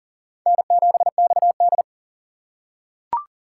Morse code, heard as a keyed tone, about 40 wpm, 700 Hz. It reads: N7XD
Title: USA Call Signs - 4x Character Spacing - 40wpm